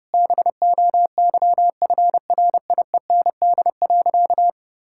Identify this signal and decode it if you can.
Morse code, heard as a keyed tone, about 30 wpm, 700 Hz. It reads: BOYFRIEND.
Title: Sentences from Top 1000 Words - Review Part 5 - 30wpm